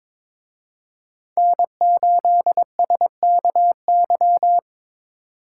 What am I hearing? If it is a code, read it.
Morse code, heard as a keyed tone, about 27 wpm, 700 Hz. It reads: N8SKY
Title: Speed Racing - USA Call Signs 22wpm